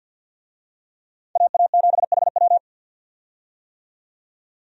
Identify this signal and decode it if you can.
Morse code, heard as a keyed tone, about 52 wpm, 700 Hz. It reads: AA7HW
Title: USA Call Signs - 5 Characters - 50wpm